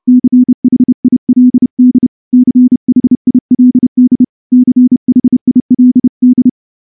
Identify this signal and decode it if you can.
Morse code, heard as a keyed tone, about 28 wpm, 250 Hz. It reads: CHILD CHILD CHILD
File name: morse-training-30WPM